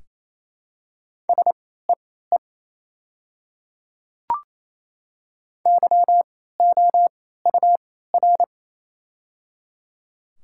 Morse code, heard as a keyed tone, about 28 wpm, spaced as for 16 wpm, 700 Hz. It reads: SEE YOUR
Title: 3x Character Spacing - Top 100 Words - 28wpm